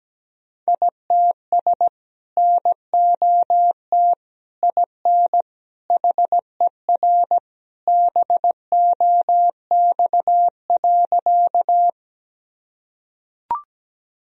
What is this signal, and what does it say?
Morse code, heard as a keyed tone, about 17 wpm, 700 Hz. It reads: ITS NOT IN HER BOX.
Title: Sentences from Top 1000 Words - Review Part 2 - 17wpm